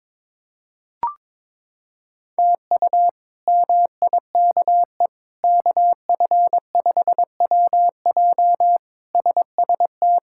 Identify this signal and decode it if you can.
Morse code, heard as a keyed tone, about 22 wpm, 700 Hz. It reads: TU MIKE KF5WJ SST
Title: K1USN SST QSO Practice - 22wpm